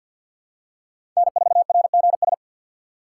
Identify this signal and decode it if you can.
Morse code, heard as a keyed tone, about 46 wpm, 700 Hz. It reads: N4RGS
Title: USA Call Signs - 3x Character Spacing - 50wpm